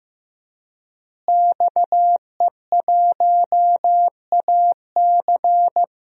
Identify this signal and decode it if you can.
Morse code, heard as a keyed tone, about 15 wpm, 700 Hz. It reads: XE1AC
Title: DXCC-100 Call Signs - 5 Characters - Repeated 3x - 15wpm